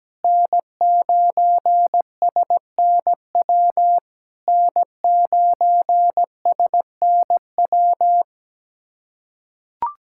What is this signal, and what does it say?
Morse code, heard as a keyed tone, about 17 wpm, 690 Hz. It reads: N9SNW N9SNW
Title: SOTA QSO Practice 17wpm